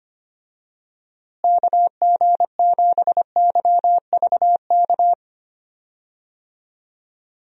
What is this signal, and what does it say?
Morse code, heard as a keyed tone, about 25 wpm, 700 Hz. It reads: KG7YVK
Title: USA Call Signs - 6 Characters - Repeated 2x - 25wpm